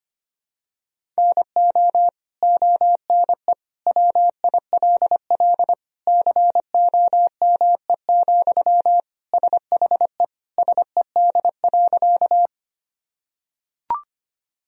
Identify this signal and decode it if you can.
Morse code, heard as a keyed tone, about 25 wpm, 700 Hz. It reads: NO ONE WILL COME, SHE SED.
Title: Sentences from Top 100 Words and Abbreviations 25wpm